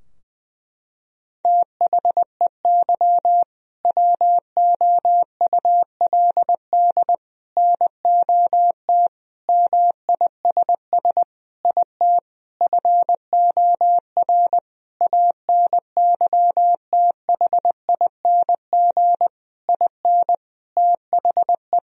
Morse code, heard as a keyed tone, about 20 wpm, 700 Hz. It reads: THEY WOULD NOT MISS IT FOR ANYTHING IN THE
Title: Sentences from Top 1000 Words - Review Part 3 - 20wpm